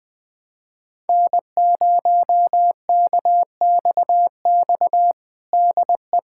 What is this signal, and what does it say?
Morse code, heard as a keyed tone, about 20 wpm, 700 Hz. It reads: N0KXX DE
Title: USA Callsign DE Callsign - 5 Characters - 20wpm